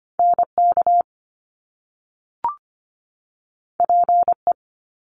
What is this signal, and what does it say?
Morse code, heard as a keyed tone, about 25 wpm, 700 Hz. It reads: NK PE
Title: Top Two Letter Combinations in Words 25wpm